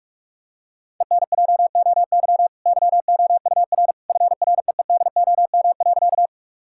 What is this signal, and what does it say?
Morse code, heard as a keyed tone, about 45 wpm, 700 Hz. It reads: ENJOY YOUR FREEDOM.
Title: Sentences from Top 1000 Words - Review Part 5 - 45wpm